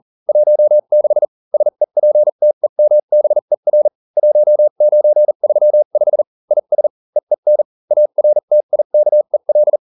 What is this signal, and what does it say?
Morse code, heard as a keyed tone, about 40 wpm, spaced as for 35 wpm, 600 Hz. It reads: 16 SEPTEMBER 1935 IS EEN ARTIKEL